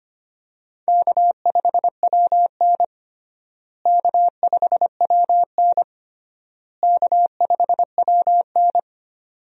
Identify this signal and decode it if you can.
Morse code, heard as a keyed tone, about 25 wpm, 700 Hz. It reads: K5WN K5WN K5WN
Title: USA Call Signs - 4 Characters - Repeated 3x - 25wpm